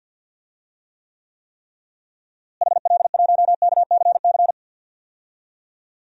Morse code, heard as a keyed tone, about 51 wpm, 700 Hz. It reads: HL1KCC
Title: DXCC-100 Call Signs - 6 Characters - Repeated 3x - 50wpm